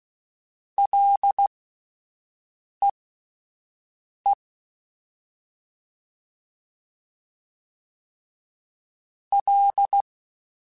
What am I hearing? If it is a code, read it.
Morse code, heard as a keyed tone, about 16 wpm, spaced as for 5 wpm, 800 Hz. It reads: LEE L